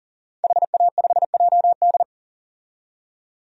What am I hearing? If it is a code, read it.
Morse code, heard as a keyed tone, about 40 wpm, 700 Hz. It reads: HA5JD
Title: DXCC-100 Call Signs - 5 Characters - 40wpm